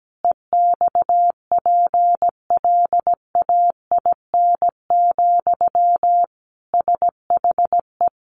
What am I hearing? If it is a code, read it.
Morse code, heard as a keyed tone, about 17 wpm, 700 Hz. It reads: EXPLAIN, SHE
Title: Sentences from Top 1100 Words 17wpm